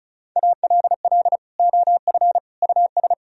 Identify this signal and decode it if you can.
Morse code, heard as a keyed tone, about 40 wpm, 700 Hz. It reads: ALL OF US